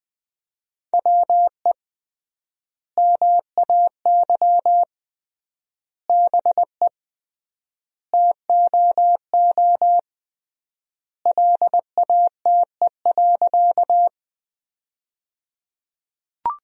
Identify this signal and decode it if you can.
Morse code, heard as a keyed tone, about 20 wpm, 700 Hz. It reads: WE MAY BE TOO LATE.
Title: 3x Word-Spacing - Sentences from Top 300 Words - 20wpm